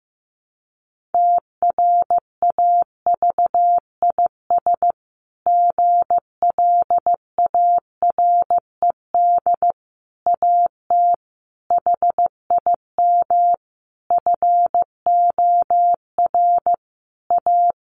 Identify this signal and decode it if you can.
Morse code, heard as a keyed tone, about 15 wpm, 700 Hz. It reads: TRAVIS GLARED AT HIM FOR A
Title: A Sound Of Thunder 15wpm